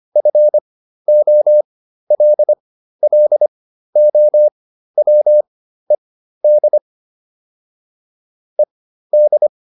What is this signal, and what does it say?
Morse code, heard as a keyed tone, about 25 wpm, spaced as for 13 wpm, 600 Hz. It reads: FOLLOWED ED